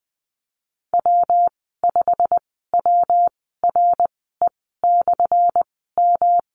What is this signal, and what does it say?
Morse code, heard as a keyed tone, about 20 wpm, spaced as for 15 wpm, 700 Hz. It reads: W5WRE/M
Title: 179 Lesson 29g - Slash - Call Signs 20wpm